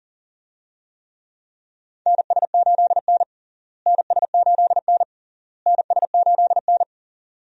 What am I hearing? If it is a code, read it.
Morse code, heard as a keyed tone, about 37 wpm, 700 Hz. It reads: NS8N NS8N NS8N
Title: USA Call Signs - 4 Characters - Repeated 3x - 40wpm